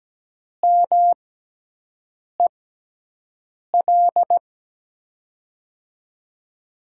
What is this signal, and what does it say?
Morse code, heard as a keyed tone, about 17 wpm, spaced as for 6 wpm, 700 Hz. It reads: MEL
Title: USA Call Signs - 6 Characters - 6x Character Spacing - 17wpm